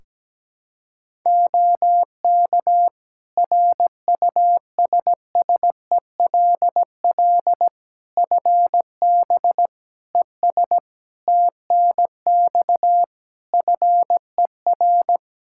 Morse code, heard as a keyed tone, about 17 wpm, 700 Hz. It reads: OK RUSSELL FB ES TNX FER
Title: Practice QSO - Exchange 2 - Info - 17wpm